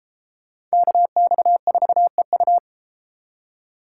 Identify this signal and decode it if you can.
Morse code, heard as a keyed tone, about 32 wpm, 700 Hz. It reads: KX4EU